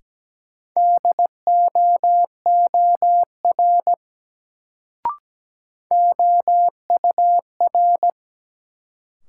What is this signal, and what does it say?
Morse code, heard as a keyed tone, about 17 wpm, 700 Hz. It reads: DOOR OUR
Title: Sets of 2 Words - New Word Drill - 17wpm